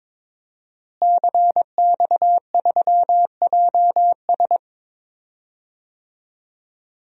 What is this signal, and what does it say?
Morse code, heard as a keyed tone, about 22 wpm, 700 Hz. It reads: CX3JS
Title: DXCC-100 Call Signs - 5 Characters - Repeated 2x - 22wpm